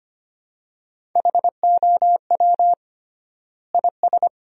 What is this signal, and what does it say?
Morse code, heard as a keyed tone, about 25 wpm, 700 Hz. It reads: HOW IS
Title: 3x Word-Spacing - Sentences from Top 500 Words - 25wpm